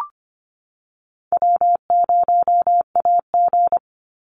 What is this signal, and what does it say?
Morse code, heard as a keyed tone, about 25 wpm, 700 Hz. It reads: W0AG
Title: Call Signs - 25wpm